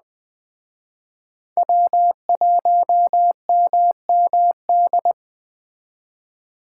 Morse code, heard as a keyed tone, about 21 wpm, 700 Hz. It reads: W1MMD
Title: Speed Racing - USA 1x3 Call Signs 17wpm